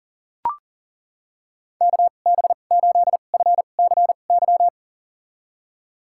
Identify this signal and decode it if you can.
Morse code, heard as a keyed tone, about 40 wpm, spaced as for 29 wpm, 690 Hz. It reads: KB8FCY
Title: USA Call Signs - 6 Characters - 2x Character Spacing - 40wpm